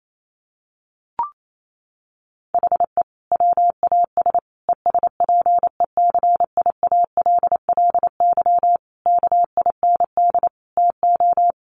Morse code, heard as a keyed tone, about 28 wpm, 700 Hz. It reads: HE WAS ESPECIALLY KIND TO
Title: Sentences from Top 700 Words 28wpm